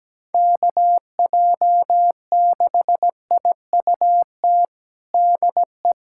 Morse code, heard as a keyed tone, about 17 wpm, 700 Hz. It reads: KJ6IUT DE
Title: USA Callsign DE Callsign 17wpm